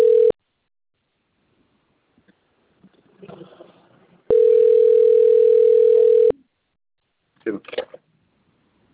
Ringback tone (North America), 440 + 480 Hz.